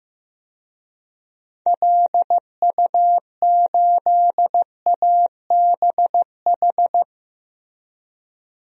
Morse code, heard as a keyed tone, about 16 wpm, 700 Hz. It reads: LU8ABH
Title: DXCC-100 Call Signs - 6 Characters - Repeated 2x - 15wpm